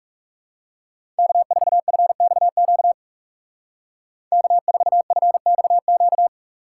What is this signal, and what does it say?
Morse code, heard as a keyed tone, about 39 wpm, 700 Hz. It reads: K4FXQ K4FXQ